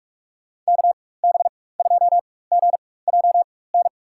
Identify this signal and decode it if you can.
Morse code, heard as a keyed tone, about 45 wpm, spaced as for 21 wpm, 700 Hz. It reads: KB2GJN
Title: USA Call Signs - 6 Characters - 4x Character Spacing - 45wpm